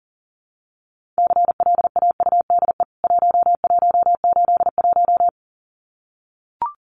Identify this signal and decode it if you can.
Morse code, heard as a keyed tone, about 37 wpm, 700 Hz. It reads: CLAUDE 1181
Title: CWT Exchange 40wpm